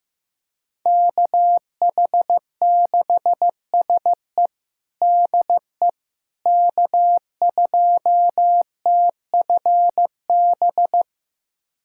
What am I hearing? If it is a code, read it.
Morse code, heard as a keyed tone, about 15 wpm, 700 Hz. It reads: KH6SE DE K2TFB
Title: USA Callsign DE Callsign - 5 Characters - 15wpm